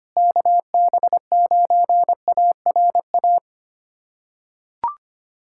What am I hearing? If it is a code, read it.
Morse code, heard as a keyed tone, about 25 wpm, 700 Hz. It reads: KB9ARA
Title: USA Callsign DE Callsign - 6 Characters - 25wpm